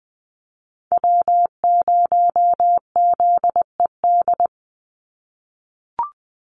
Morse code, heard as a keyed tone, about 20 wpm, 700 Hz. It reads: W0ZED
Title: USA Call Signs 20wpm